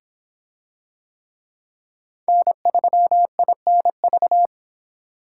Morse code, heard as a keyed tone, about 24 wpm, 700 Hz. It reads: N3INV